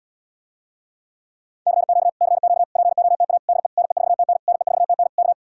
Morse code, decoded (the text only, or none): CQ CQ POTA DE NE4TN NE4TN K